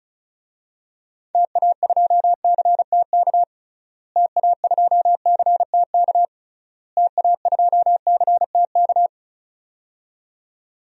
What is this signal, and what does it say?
Morse code, heard as a keyed tone, about 36 wpm, 700 Hz. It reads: TA2CTK TA2CTK TA2CTK
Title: DXCC-100 Call Signs - 6 Characters - Repeated 3x - 35wpm